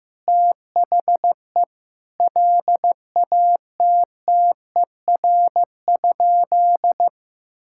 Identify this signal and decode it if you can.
Morse code, heard as a keyed tone, about 15 wpm, 700 Hz. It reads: THE LATTER?
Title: Sentences from Top 1600 Words 15wpm